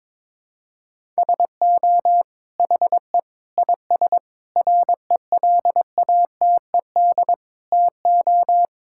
Morse code, heard as a keyed tone, about 22 wpm, 700 Hz. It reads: SO HE IS RELATED TO
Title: Sentences from Top 1000 Words - Review Part 5 - 22wpm